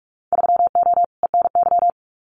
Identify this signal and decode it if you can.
Morse code, heard as a keyed tone, about 44 wpm, 690 Hz. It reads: 3O ENY
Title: Winter Field Day Exchanges Encore 45wpm